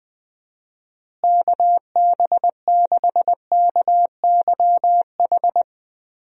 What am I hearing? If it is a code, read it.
Morse code, heard as a keyed tone, about 20 wpm, 700 Hz. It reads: KB6KYH
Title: USA Call Signs - 6 Characters - Repeated 2x - 20wpm